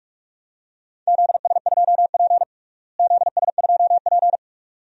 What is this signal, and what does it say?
Morse code, heard as a keyed tone, about 44 wpm, 700 Hz. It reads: ZS2P ZS2P
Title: DXCC-100 Call Signs - 4 Characters - Repeated 2x - 45wpm